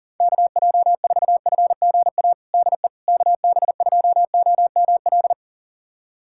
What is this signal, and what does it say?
Morse code, heard as a keyed tone, about 37 wpm, 700 Hz. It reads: KJ4FGA DE KB2OML
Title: USA Callsign DE Callsign - 6 Characters - 40wpm